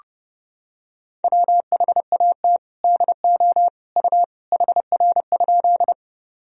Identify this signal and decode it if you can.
Morse code, heard as a keyed tone, about 30 wpm, 700 Hz. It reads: WHAT DO U HR?
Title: Sentences from Top 300 Words and Abbreviations 30wpm